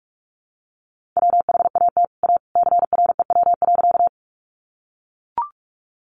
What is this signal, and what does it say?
Morse code, heard as a keyed tone, about 44 wpm, 700 Hz. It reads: WHAT A CREW.
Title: Sentences from Top 1500 Words 45wpm